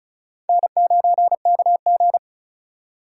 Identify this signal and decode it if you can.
Morse code, heard as a keyed tone, about 42 wpm, 700 Hz. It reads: N9KG
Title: Speed Racing - USA Call Signs 35wpm